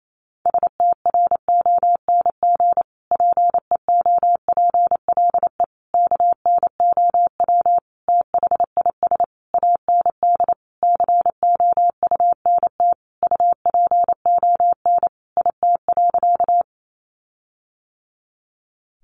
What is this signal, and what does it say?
Morse code, heard as a keyed tone, about 28 wpm, 700 Hz. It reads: STRONG PEOPLE KNOW THIS AND COUNT UPON IT.
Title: Sentences from Top 1500 Words 28wpm